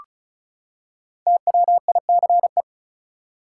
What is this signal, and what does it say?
Morse code, heard as a keyed tone, about 34 wpm, 700 Hz. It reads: TWICE
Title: Five Letter Words - 35wpm